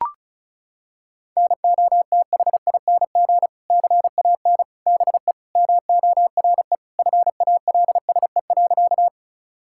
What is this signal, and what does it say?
Morse code, heard as a keyed tone, about 36 wpm, 700 Hz. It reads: NOTHING CAN BE MORE FALSE.
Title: Sentences from Top 1600 Words 35wpm